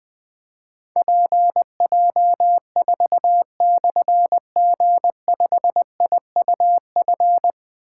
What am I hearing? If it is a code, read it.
Morse code, heard as a keyed tone, about 20 wpm, 700 Hz. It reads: PJ4/G5IUF
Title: Difficult Call Signs 20wpm